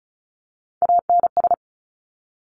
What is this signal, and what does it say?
Morse code, heard as a keyed tone, about 36 wpm, 700 Hz. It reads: ANS
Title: Top Three Letter Combinations in Words 35wpm